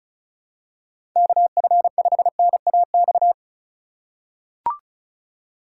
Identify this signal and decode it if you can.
Morse code, heard as a keyed tone, about 34 wpm, 700 Hz. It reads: KF5NAX
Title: USA Call Signs - 6 Characters - 2x Character Spacing - 35wpm